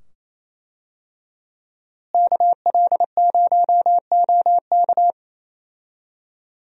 Morse code, heard as a keyed tone, about 28 wpm, 700 Hz. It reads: KL0OK
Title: USA Call Signs - 5 Characters - Repeated 2x - 28wpm